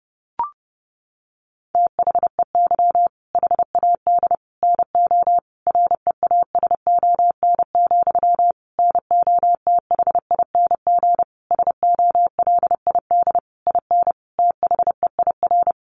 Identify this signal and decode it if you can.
Morse code, heard as a keyed tone, about 30 wpm, 700 Hz. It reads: THEY HAD NO REASON, NOTHING SOLID IN THEIR
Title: Sentences from Top 1500 Words 30wpm